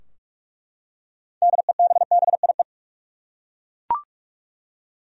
Morse code, heard as a keyed tone, about 47 wpm, 700 Hz. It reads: DEBBIE
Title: USA Common Names - 45wpm